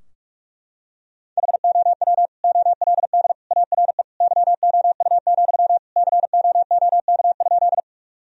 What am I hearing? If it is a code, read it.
Morse code, heard as a keyed tone, about 44 wpm, 700 Hz. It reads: HOW OLD ARE YOU, COOK?
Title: Sentences from Top 1000 Words - Review Part 4 - 45wpm